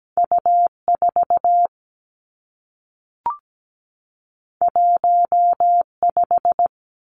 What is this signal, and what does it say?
Morse code, heard as a keyed tone, about 17 wpm, 700 Hz. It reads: U4 15
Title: Two Challenging Letter-Number Combinations 17wpm